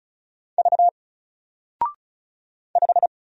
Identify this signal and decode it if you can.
Morse code, heard as a keyed tone, about 36 wpm, 700 Hz. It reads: V 5
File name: Single-Letter-Number-35wpm